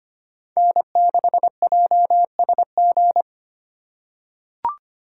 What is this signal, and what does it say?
Morse code, heard as a keyed tone, about 25 wpm, 700 Hz. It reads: N6JSG
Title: USA Call Signs - 5 Characters - 4x Character Spacing - 25wpm